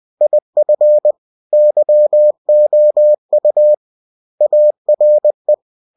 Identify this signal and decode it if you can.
Morse code, heard as a keyed tone, about 20 wpm, 600 Hz. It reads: IF YOU ARE
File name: winnie-20wpm